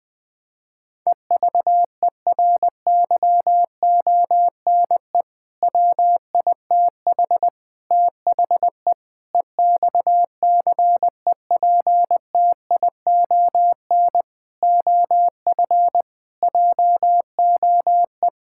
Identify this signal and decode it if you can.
Morse code, heard as a keyed tone, about 21 wpm, 700 Hz. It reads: EVERYONE WITH THE EXCEPTION OF JOE
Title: Sentences with 4-Word Phrases - 20wpm